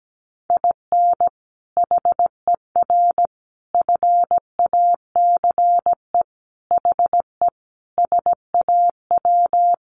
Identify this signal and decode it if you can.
Morse code, heard as a keyed tone, about 17 wpm, 700 Hz. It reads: IN HER FACE HE SAW
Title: Sentences from Top 1300 Words 17wpm